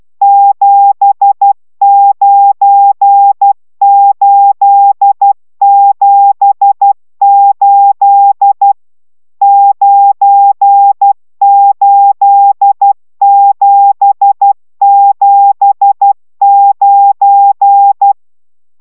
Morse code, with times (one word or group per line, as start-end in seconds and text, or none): 0.2-8.7 79878
9.4-18.1 98779